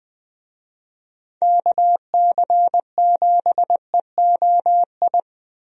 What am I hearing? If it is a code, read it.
Morse code, heard as a keyed tone, about 20 wpm, 700 Hz. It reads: KC7EOI
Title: USA Call Signs - 6 Characters - Repeated 2x - 20wpm